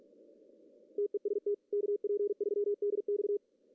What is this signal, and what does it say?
Morse code, heard as a keyed tone, about 46 wpm, 410 Hz. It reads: TEST KP3DX
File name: CW / cqww-cw-k5tr